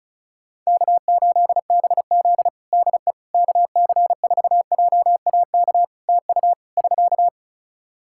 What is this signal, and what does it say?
Morse code, heard as a keyed tone, about 36 wpm, 700 Hz. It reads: K8BZ DE KC4JAK TU <SK>
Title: Practice QSO - Exchange 4 - Ending - 35wpm